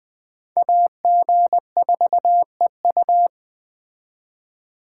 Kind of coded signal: Morse code

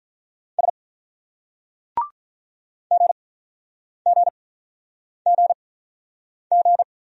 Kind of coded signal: Morse code

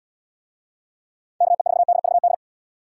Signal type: Morse code

